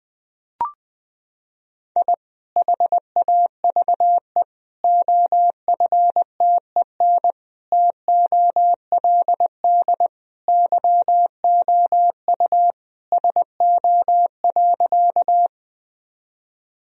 Morse code, heard as a keyed tone, about 21 wpm, 700 Hz. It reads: I HAVE OFTEN TOLD YOU SO.
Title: Sentences from Top 1000 Words - Review Part 1 - 20wpm